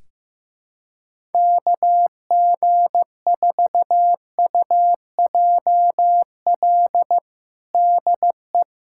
Morse code, heard as a keyed tone, about 15 wpm, 700 Hz. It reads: KG4UJL DE
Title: USA Callsign DE Callsign 15wpm